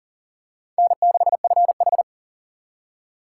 Morse code, heard as a keyed tone, about 40 wpm, 700 Hz. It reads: N6FH
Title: USA Call Signs - 4 Characters - 2x Character Spacing - 40wpm